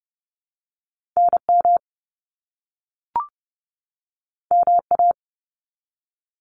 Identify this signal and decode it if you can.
Morse code, heard as a keyed tone, about 30 wpm, 700 Hz. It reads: NM MA
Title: US States 30wpm